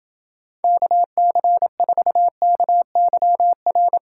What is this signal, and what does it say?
Morse code, heard as a keyed tone, about 31 wpm, 700 Hz. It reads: KC4KYR